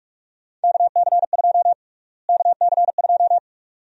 Morse code, heard as a keyed tone, about 45 wpm, 700 Hz. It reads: KC2 KC2
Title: USA Call Sign Prefixes - Repeated 2x - 45wpm